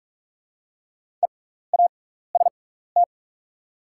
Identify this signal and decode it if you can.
Morse code, heard as a keyed tone, about 45 wpm, spaced as for 15 wpm, 700 Hz. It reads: EAST
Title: 6x Character Spacing - Top 500 Words - 45wpm